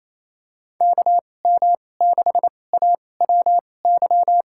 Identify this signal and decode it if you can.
Morse code, heard as a keyed tone, about 28 wpm, spaced as for 20 wpm, 700 Hz. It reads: KM6AWY
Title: USA Call Signs - 6 Characters - 2x Character Spacing - 28wpm